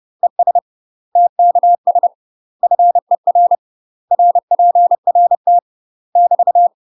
Morse code, heard as a keyed tone, about 30 wpm, 700 Hz. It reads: ES TKS FER RPRT <BT>
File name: UK MASTER QSOs TRAINING / 30WPM / UK MASTER QSOs 0005